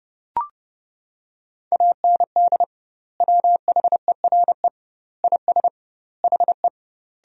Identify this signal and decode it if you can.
Morse code, heard as a keyed tone, about 30 wpm, 700 Hz. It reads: AND WHERE IS HE
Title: 2x Word-Spacing - Sentences from Top 200 Words - 30wpm